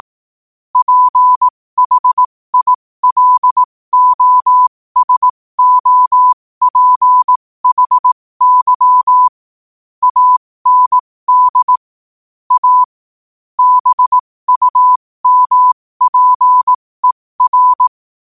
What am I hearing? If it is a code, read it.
Morse code, heard as a keyed tone, about 18 wpm, spaced as for 15 wpm, 1000 Hz. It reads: PHILOSOPHY AND A BUMPER
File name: qotdfiles / 15wpm